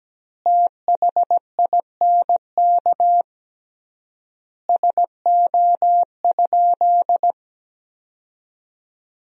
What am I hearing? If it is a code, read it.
Morse code, heard as a keyed tone, about 17 wpm, 700 Hz. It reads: THINK SO?
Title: 3x Word-Spacing - Sentences from Top 200 Words - 17wpm